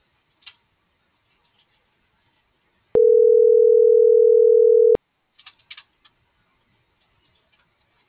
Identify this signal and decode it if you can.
Ringback tone (North America), 440 + 480 Hz.